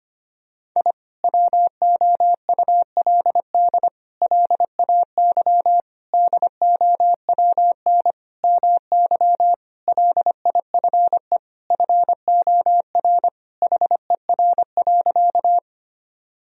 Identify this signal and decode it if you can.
Morse code, heard as a keyed tone, about 25 wpm, 700 Hz. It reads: I WOULD LAY DOWN MY LIFE FOR HER.
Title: Sentences from Top 1000 Words - Review Part 2 - 25wpm